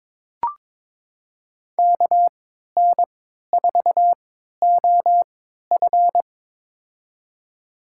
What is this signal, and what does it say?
Morse code, heard as a keyed tone, about 22 wpm, spaced as for 13 wpm, 700 Hz. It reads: KN4OF
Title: USA Call Signs - 5 Characters - 3x Character Spacing - 22wpm